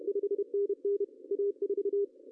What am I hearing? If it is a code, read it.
Morse code, heard as a keyed tone, about 32 wpm, 400 Hz. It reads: 5NN A4
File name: CW / cqww-cw-k5tr